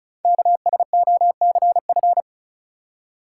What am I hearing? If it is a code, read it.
Morse code, heard as a keyed tone, about 36 wpm, 700 Hz. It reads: KSOCF
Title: Code Groups - 5 Letter:Numbers - 35wpm